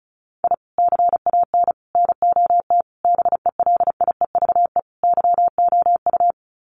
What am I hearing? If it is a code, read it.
Morse code, heard as a keyed tone, about 34 wpm, 700 Hz. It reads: I CAN NOT BELIEVE YOU
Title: Sets of 5 Words Encore 35wpm